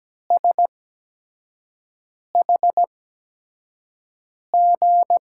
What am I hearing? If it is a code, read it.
Morse code, heard as a keyed tone, about 17 wpm, spaced as for 5 wpm, 700 Hz. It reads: SHG
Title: USA Call Signs - 5 Characters - 8x Character Spacing - 17wpm